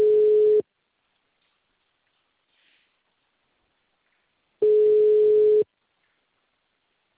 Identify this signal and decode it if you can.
Ringback tone (Europe, 425 Hz).
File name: out- -2020- - - .10044